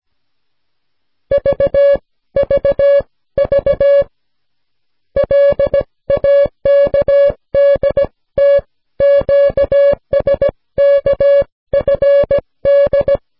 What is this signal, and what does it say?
Morse code, heard as a keyed tone, about 17 wpm, spaced as for 14 wpm, 560 Hz. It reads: VVV LAKDT QSKFD